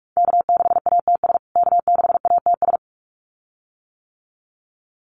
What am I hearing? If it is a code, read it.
Morse code, heard as a keyed tone, about 44 wpm, 700 Hz. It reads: K6ATS K6ATS
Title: USA Call Signs Repeated 2x 45wpm